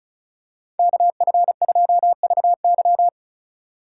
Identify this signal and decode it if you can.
Morse code, heard as a keyed tone, about 35 wpm, 700 Hz. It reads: KF2VY
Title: USA Call Signs - 5 Characters - 2x Character Spacing - 35wpm